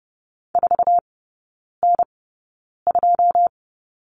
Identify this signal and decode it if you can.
Morse code, heard as a keyed tone, about 30 wpm, spaced as for 9 wpm, 700 Hz. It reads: 4N2
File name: 081 Lesson 14e - Number 2 - Three Characters 30wpm